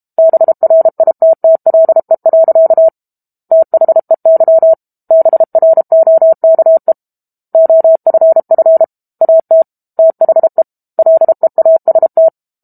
Morse code, heard as a keyed tone, about 32 wpm, 650 Hz. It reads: BRITTLE. THEY BROKE OFF AT THE LEAST
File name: Book Two / 06-Chapter 6 - 32 WPM